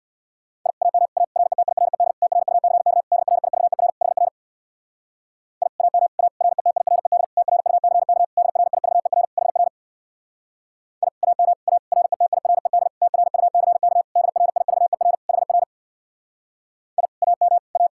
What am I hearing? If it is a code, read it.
Morse code, heard as a keyed tone, about 65 wpm, 700 Hz. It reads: I AM A RETIRED TRUCK DRIVER HR I AM A RETIRED TRUCK DRIVER HR I AM A RETIRED TRUCK DRIVER HR I AM A